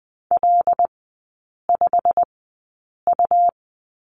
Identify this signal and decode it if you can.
Morse code, heard as a keyed tone, about 20 wpm, spaced as for 8 wpm, 700 Hz. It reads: L5U
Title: 127 Lesson 21e - Number 6 - Three Characters 20wpm